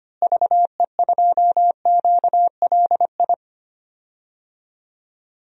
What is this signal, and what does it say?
Morse code, heard as a keyed tone, about 25 wpm, 700 Hz. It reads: VE2QLI
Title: DXCC-100 Call Signs - 6 Characters - Repeated 3x - 25wpm